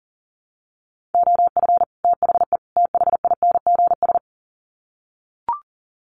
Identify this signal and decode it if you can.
Morse code, heard as a keyed tone, about 40 wpm, 700 Hz. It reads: OF THE THINGS
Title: Sets of 3 Words 40wpm